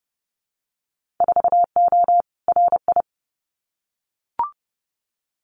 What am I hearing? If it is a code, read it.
Morse code, heard as a keyed tone, about 30 wpm, 700 Hz. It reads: 4O RI
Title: Winter Field Day Exchanges Encore 30wpm